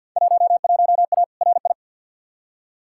Morse code, heard as a keyed tone, about 52 wpm, 700 Hz. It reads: 11A RI
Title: Field Day Exchanges Encore - 50wpm